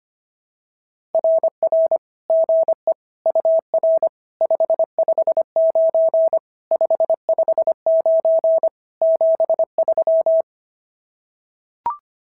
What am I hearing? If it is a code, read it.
Morse code, heard as a keyed tone, about 25 wpm, 660 Hz. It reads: RR GE UR 559 559 73
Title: SOTA QSO Practice 25wpm